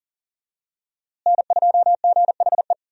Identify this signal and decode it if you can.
Morse code, heard as a keyed tone, about 40 wpm, 700 Hz. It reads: N2GHE
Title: USA Call Signs - 5 Characters - Repeated 3x - 40wpm